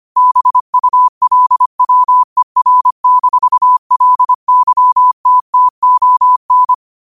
Morse code, heard as a keyed tone, about 25 wpm, 1000 Hz. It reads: BULWER-LYTTON
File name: qotdfiles / 25wpm